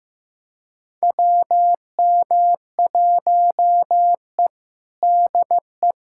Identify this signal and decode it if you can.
Morse code, heard as a keyed tone, about 15 wpm, 700 Hz. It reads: WM1E DE
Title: USA Callsign DE Callsign - 4 Characters - 15wpm